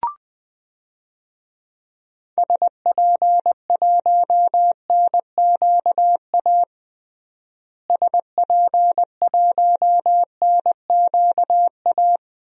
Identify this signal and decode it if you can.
Morse code, heard as a keyed tone, about 20 wpm, 700 Hz. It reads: SP1NQA SP1NQA
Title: DXCC-100 Call Signs - 6 Characters - Repeated 3x - 20wpm